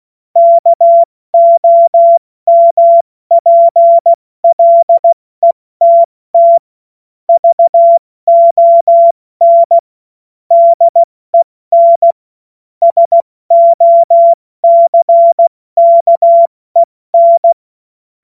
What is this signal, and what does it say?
Morse code, heard as a keyed tone, about 16 wpm, spaced as for 14 wpm, 680 Hz. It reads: KOMPLETT VON DEN SOCKEN
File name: Morsekurs / Witze / Witz 21. Dezember 2021, 70 F80